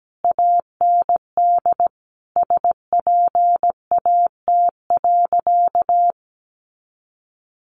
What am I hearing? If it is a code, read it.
Morse code, heard as a keyed tone, about 17 wpm, 700 Hz. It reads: AND SPAT.
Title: A Sound Of Thunder 17wpm